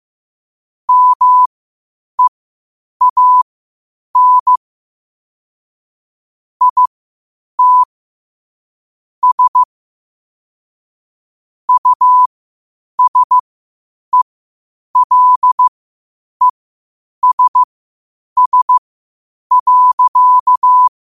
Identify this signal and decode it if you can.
Morse code, heard as a keyed tone, about 15 wpm, spaced as for 8 wpm, 1000 Hz. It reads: MEAN IT S USELESS.
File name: qotdfiles / 8wpm